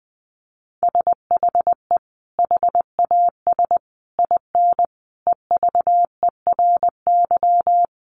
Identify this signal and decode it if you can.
Morse code, heard as a keyed tone, about 20 wpm, 700 Hz. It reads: SHE HAS IN EVERY
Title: Sentences from Top 1300 Words 20wpm